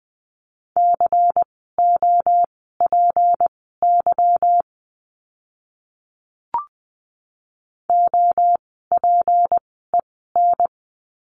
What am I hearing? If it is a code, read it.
Morse code, heard as a keyed tone, about 20 wpm, spaced as for 15 wpm, 700 Hz. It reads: COPY OPEN
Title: 153 Lesson 25f - Letter p - Words 20wpm